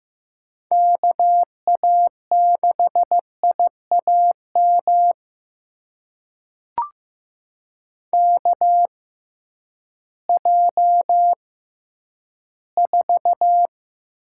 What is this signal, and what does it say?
Morse code, heard as a keyed tone, about 15 wpm, spaced as for 10 wpm, 700 Hz. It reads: KA6IAM KJ4
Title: USA Call Signs - 6 Characters - 6x Character Spacing - 15wpm